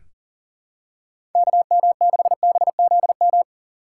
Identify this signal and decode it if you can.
Morse code, heard as a keyed tone, about 40 wpm, 700 Hz. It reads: KM6BZM
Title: USA Call Signs - 6 Characters - 4x Character Spacing - 40wpm